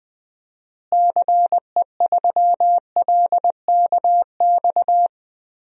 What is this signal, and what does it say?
Morse code, heard as a keyed tone, about 20 wpm, 700 Hz. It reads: CE3LKX
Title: DXCC-100 Call Signs - 6 Characters - Repeated 2x - 20wpm